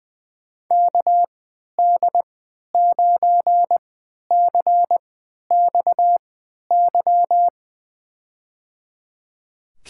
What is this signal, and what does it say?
Morse code, heard as a keyed tone, about 20 wpm, spaced as for 11 wpm, 700 Hz. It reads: KD9CXY
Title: USA Call Signs - 6 Characters - 3x Character Spacing - 20wpm